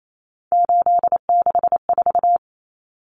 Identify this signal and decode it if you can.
Morse code, heard as a keyed tone, about 28 wpm, 700 Hz. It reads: 864
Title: Common Cut Numbers - 3-Digit 28wpm